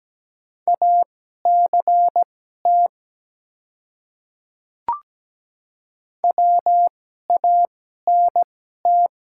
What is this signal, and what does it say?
Morse code, heard as a keyed tone, about 17 wpm, spaced as for 12 wpm, 700 Hz. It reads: ACT WANT
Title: 2x Character Spacing - Top 300 Words - 17wpm